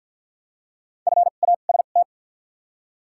Morse code, heard as a keyed tone, about 48 wpm, spaced as for 35 wpm, 700 Hz. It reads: FAST
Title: 2x Character Spacing - Top 500 Words - 50wpm